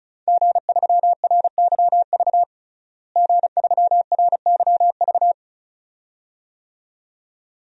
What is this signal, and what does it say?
Morse code, heard as a keyed tone, about 36 wpm, 700 Hz. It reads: G3RYV G3RYV
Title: DXCC-100 Call Signs - 5 Characters - Repeated 2x - 35wpm